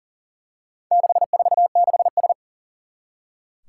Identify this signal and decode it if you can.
Morse code, heard as a keyed tone, about 40 wpm, 700 Hz. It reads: 646S
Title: Four Challenging Letter-Number Combinations 40wpm